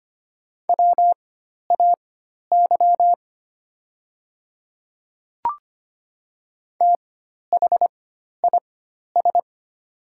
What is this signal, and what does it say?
Morse code, heard as a keyed tone, about 25 wpm, spaced as for 12 wpm, 700 Hz. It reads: WAY THIS
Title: 4x Character Spacing - Top 100 Words - 25wpm